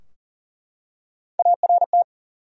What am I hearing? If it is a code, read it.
Morse code, heard as a keyed tone, about 40 wpm, 700 Hz. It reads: ART
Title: USA Common Names - Max 3 Letters - 40wpm